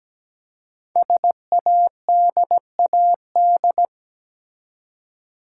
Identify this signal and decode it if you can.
Morse code, heard as a keyed tone, about 17 wpm, 700 Hz. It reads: SADAD
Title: Code Groups - 5 Letters - 17wpm